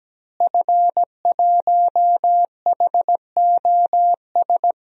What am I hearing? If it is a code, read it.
Morse code, heard as a keyed tone, about 17 wpm, 700 Hz. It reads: F1HOS
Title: DXCC-100 Call Signs - 5 Characters - Repeated 2x - 17wpm